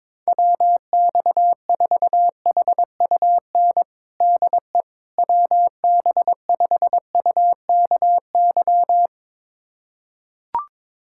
Morse code, heard as a keyed tone, about 22 wpm, 700 Hz. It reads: WX4HUN DE WB5UKY
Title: USA Callsign DE Callsign - 6 Characters - 22wpm